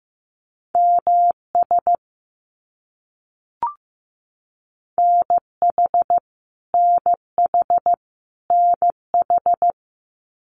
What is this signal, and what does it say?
Morse code, heard as a keyed tone, about 15 wpm, 700 Hz. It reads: MS NH NH NH
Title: US States Repeated 3x 15wpm